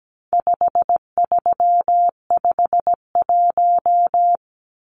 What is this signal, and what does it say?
Morse code, heard as a keyed tone, about 17 wpm, 700 Hz. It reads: 5351
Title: Numbers - 4-Digit 17wpm